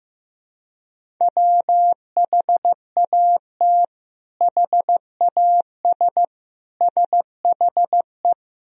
Morse code, heard as a keyed tone, about 15 wpm, 700 Hz. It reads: WHAT HAS SHE
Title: Sentences from Top 100 Words and Abbreviations 15wpm